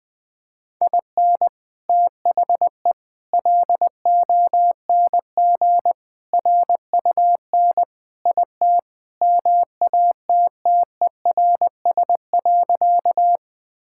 Morse code, heard as a keyed tone, about 21 wpm, 700 Hz. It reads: IN THE LONG RUN IT MATTERS.
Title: Sentences with 4-Word Phrases - 20wpm